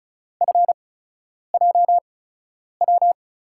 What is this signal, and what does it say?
Morse code, heard as a keyed tone, about 34 wpm, spaced as for 10 wpm, 700 Hz. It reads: FJW
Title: USA Call Signs - 6 Characters - 8x Character Spacing - 35wpm